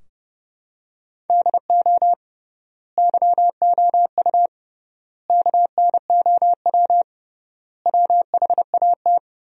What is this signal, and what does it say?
Morse code, heard as a keyed tone, about 32 wpm, 700 Hz. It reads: DO YOU KNOW WHAT
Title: 3x Word-Spacing - Sentences from Top 100 Words - 30wpm